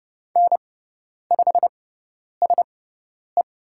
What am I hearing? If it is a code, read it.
Morse code, heard as a keyed tone, about 30 wpm, spaced as for 10 wpm, 700 Hz. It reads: N5SE
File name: Lesson 16g - Letter c - Call Signs 30wpm